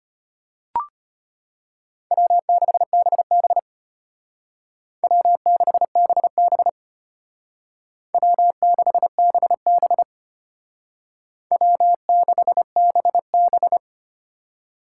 Morse code, heard as a keyed tone, about 32 wpm, 700 Hz. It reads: W6BB W6BB W6BB W6BB